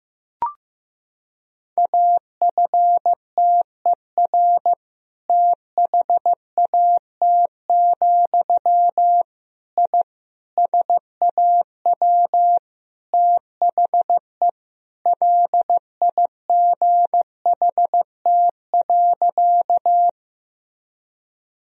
Morse code, heard as a keyed tone, about 16 wpm, 700 Hz. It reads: AFTER THAT, I SAW THE LIGHT.
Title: Sentences from Top 1000 Words - Review Part 1 - 15wpm